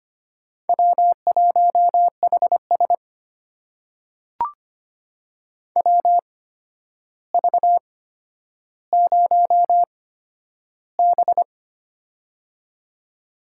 Morse code, heard as a keyed tone, about 25 wpm, spaced as for 11 wpm, 700 Hz. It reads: W1HS WV0B
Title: USA Call Signs - 4 Characters - 8x Character Spacing - 25wpm